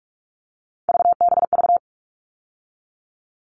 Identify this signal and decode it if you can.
Morse code, heard as a keyed tone, about 44 wpm, 700 Hz. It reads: VBV
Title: All Cut Numbers - 3-Digit 45wpm